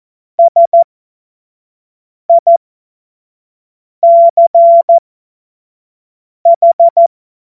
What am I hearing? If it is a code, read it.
Morse code, heard as a keyed tone, about 14 wpm, spaced as for 5 wpm, 680 Hz. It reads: SICH
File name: Morsekurs / Witze / Witz 05 März 2024 25 F70